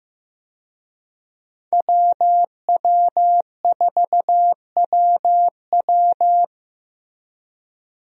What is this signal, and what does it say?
Morse code, heard as a keyed tone, about 15 wpm, 700 Hz. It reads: WW4WW
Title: USA Call Signs - 5 Characters - Repeated 2x - 15wpm